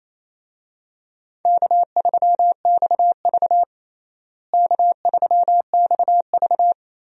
Morse code, heard as a keyed tone, about 28 wpm, 700 Hz. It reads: K3XV K3XV
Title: USA Call Signs - 4 Characters - Repeated 2x - 28wpm